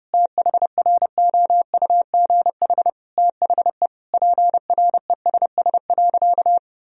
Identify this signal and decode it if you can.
Morse code, heard as a keyed tone, about 32 wpm, 700 Hz. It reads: THROUGH THE PRESS.
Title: Sentences from Top 1000 Words - Review Part 1 - 30wpm